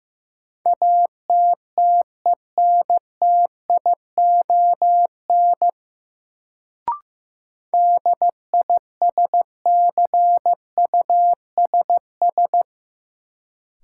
Morse code, heard as a keyed tone, about 15 wpm, 700 Hz. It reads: ATTENTION DISCUSS
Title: Sentences from Top 1000 Words - New Word Drill - 15wpm